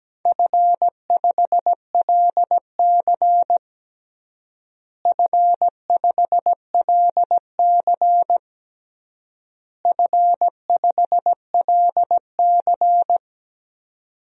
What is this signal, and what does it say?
Morse code, heard as a keyed tone, about 17 wpm, 700 Hz. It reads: F5LC F5LC F5LC
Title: DXCC-100 Call Signs - 4 Characters - Repeated 3x - 17wpm